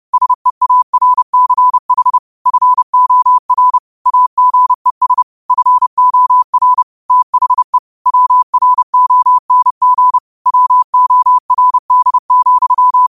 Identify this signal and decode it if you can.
Morse code, heard as a keyed tone, about 30 wpm, 1000 Hz. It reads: SEARCH FOR AGES FOR THE WRONG WORD,